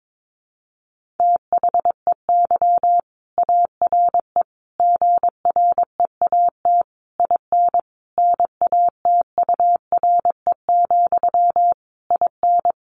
Morse code, heard as a keyed tone, about 22 wpm, 700 Hz. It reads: THEY ARE GREAT IN NATURE, IN
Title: Sentences from Top 1400 Words 22wpm